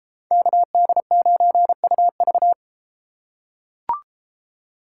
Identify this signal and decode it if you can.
Morse code, heard as a keyed tone, about 32 wpm, 700 Hz. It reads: KD9UV